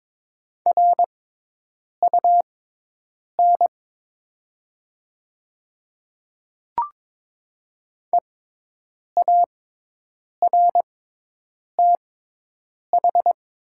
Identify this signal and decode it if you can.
Morse code, heard as a keyed tone, about 22 wpm, spaced as for 8 wpm, 700 Hz. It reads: RUN EARTH
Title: 6x Character Spacing - Top 300 Words - 22wpm